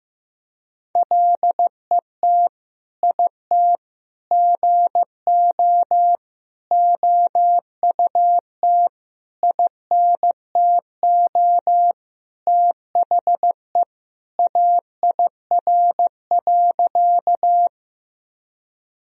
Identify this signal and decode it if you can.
Morse code, heard as a keyed tone, about 15 wpm, 700 Hz. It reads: LET IT GO OUT INTO THE AIR.
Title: Sentences from Top 1000 Words - Review Part 1 - 15wpm